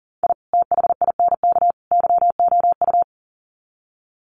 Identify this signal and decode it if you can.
Morse code, heard as a keyed tone, about 40 wpm, 700 Hz. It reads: I THINK YOU
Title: Sets of 3 Words 40wpm